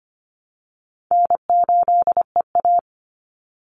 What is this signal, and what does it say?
Morse code, heard as a keyed tone, about 25 wpm, 700 Hz. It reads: N8EA
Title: Call Signs Repeated 2x - 25wpm